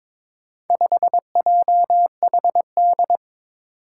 Morse code, heard as a keyed tone, about 22 wpm, 700 Hz. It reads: 5JHD
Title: Four Challenging Letter-Number Combinations 22wpm